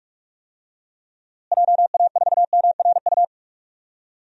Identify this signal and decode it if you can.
Morse code, heard as a keyed tone, about 41 wpm, 700 Hz. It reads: JA4MRU
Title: DXCC-100 Call Signs - 6 Characters - Repeated 2x - 45wpm